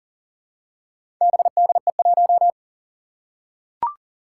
Morse code, heard as a keyed tone, about 40 wpm, 700 Hz. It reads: BDE1
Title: Four Challenging Letter-Number Combinations 40wpm